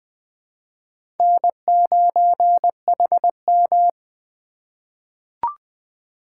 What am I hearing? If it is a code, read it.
Morse code, heard as a keyed tone, about 20 wpm, 700 Hz. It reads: N9HM
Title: USA Call Signs - 4 Characters - 4x Character Spacing - 20wpm